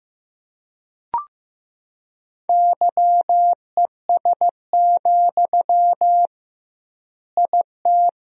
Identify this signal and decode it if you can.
Morse code, heard as a keyed tone, about 15 wpm, 700 Hz. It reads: YES, IT
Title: 2x Word-Spacing - Sentences from Top 500 Words - 15wpm